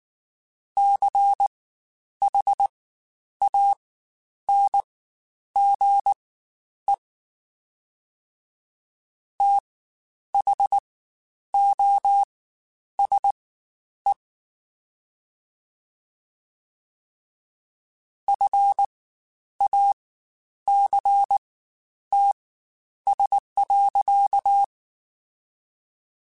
Morse code, heard as a keyed tone, about 19 wpm, spaced as for 9 wpm, 800 Hz. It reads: CHANGE THOSE FACTS.
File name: Quotes Spaced / Quotes Spaced-mp3s / IgnoringFacts sp @18